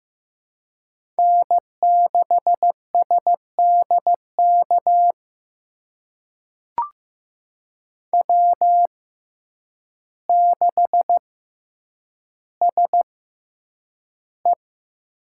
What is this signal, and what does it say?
Morse code, heard as a keyed tone, about 15 wpm, spaced as for 8 wpm, 700 Hz. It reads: N6SDK W6SE
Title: USA Call Signs - 6x Character Spacing - 15wpm